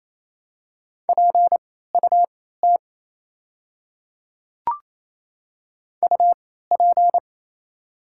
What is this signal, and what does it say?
Morse code, heard as a keyed tone, about 28 wpm, spaced as for 16 wpm, 700 Hz. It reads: PUT UP
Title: 3x Character Spacing - Top 200 Words - 28wpm